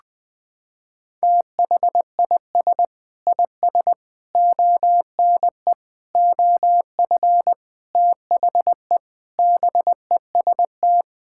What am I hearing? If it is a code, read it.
Morse code, heard as a keyed tone, about 20 wpm, 700 Hz. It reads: THIS IS ONE OF THE BEST
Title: Sentences with 4-Word Phrases - 20wpm